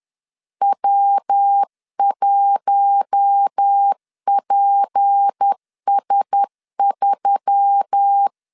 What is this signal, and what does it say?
Morse code, heard as a keyed tone, about 10 wpm, 800 Hz. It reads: W1PS3